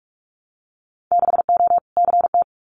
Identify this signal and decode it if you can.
Morse code, heard as a keyed tone, about 46 wpm, 700 Hz. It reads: 6O CT
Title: Winter Field Day Exchanges Encore 45wpm